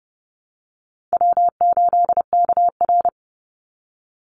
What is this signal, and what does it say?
Morse code, heard as a keyed tone, about 30 wpm, 700 Hz. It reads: W8KR
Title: Easy Call Signs - 30wpm